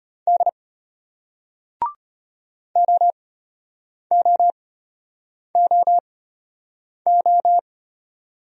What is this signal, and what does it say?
Morse code, heard as a keyed tone, about 33 wpm, spaced as for 8 wpm, 700 Hz. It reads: D OOOO